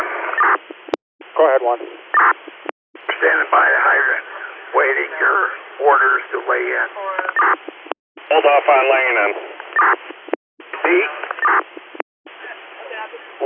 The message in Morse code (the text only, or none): none